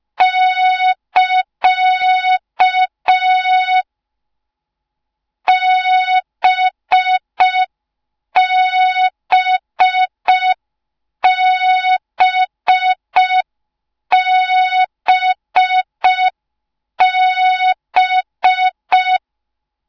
Morse code, heard as a keyed tone, about 5 wpm, 3700 Hz, with noise. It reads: <KA> BBBBB